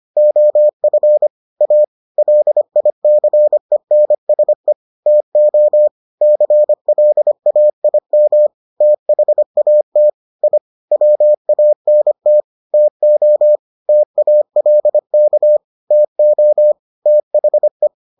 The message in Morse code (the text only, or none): OF A LICENSE TO CLAIM THAT I WANT TO TALK TO THE